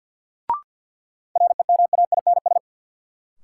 Morse code, heard as a keyed tone, about 50 wpm, 700 Hz. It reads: REMAINS
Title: Sets of 5 Words Encore - New Word Drill - 50wpm